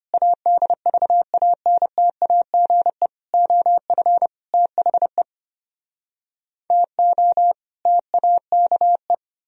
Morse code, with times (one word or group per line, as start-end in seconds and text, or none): 0.1-3.1 ADVANTAGE
3.3-4.3 OF
4.5-5.2 THE
6.7-7.5 TO
7.9-9.1 TAKE